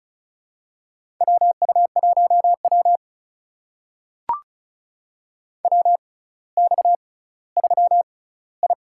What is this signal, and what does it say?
Morse code, heard as a keyed tone, about 36 wpm, spaced as for 18 wpm, 700 Hz. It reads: WU1W WX3I
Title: USA Call Signs - 4 Characters - 6x Character Spacing - 35wpm